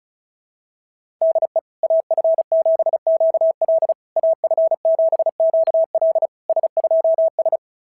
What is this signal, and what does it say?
Morse code, heard as a keyed tone, about 33 wpm, 650 Hz. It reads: DE AF7QL AF7QL S2S
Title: SOTA QSO Practice 35wpm